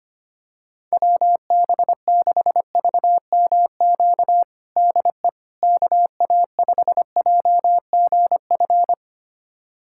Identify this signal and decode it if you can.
Morse code, heard as a keyed tone, about 25 wpm, 700 Hz. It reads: WB6VMQ DE KA5JGF
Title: USA Callsign DE Callsign - 6 Characters - 25wpm